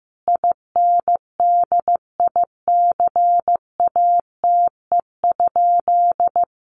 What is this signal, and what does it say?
Morse code, heard as a keyed tone, about 15 wpm, 700 Hz. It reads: INDICATE?
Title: Sentences from Top 1300 Words 15wpm